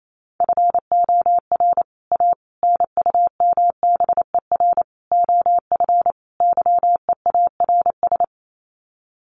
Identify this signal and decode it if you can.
Morse code, heard as a keyed tone, about 28 wpm, 700 Hz. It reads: FOR A NUMBER OF YEARS
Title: Sets of 5 Words Encore 28wpm